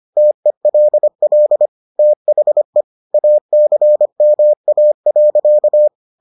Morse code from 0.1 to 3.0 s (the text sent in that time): TELL THE